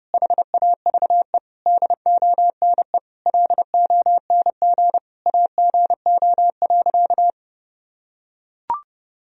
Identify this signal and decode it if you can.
Morse code, heard as a keyed tone, about 30 wpm, 700 Hz. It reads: HAVE DONE LONG AGO.
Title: Sentences from Top 1000 Words - Review Part 3 - 30wpm